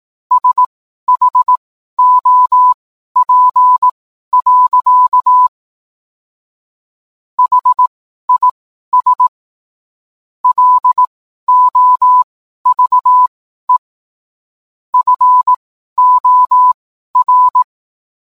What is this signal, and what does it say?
Morse code, heard as a keyed tone, about 18 wpm, spaced as for 12 wpm, 1000 Hz. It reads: SHOP. HIS LOVE FOR